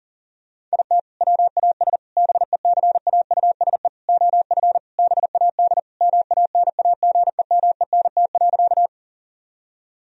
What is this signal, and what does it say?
Morse code, heard as a keyed tone, about 40 wpm, 700 Hz. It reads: IT WAS BECAUSE OF BAD MANAGEMENT.
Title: Sentences from Top 1000 Words - Review Part 4 - 40wpm